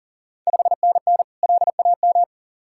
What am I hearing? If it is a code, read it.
Morse code, heard as a keyed tone, about 40 wpm, 700 Hz. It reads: 5NN LAM
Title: Ontario QSO Party Exchange 40wpm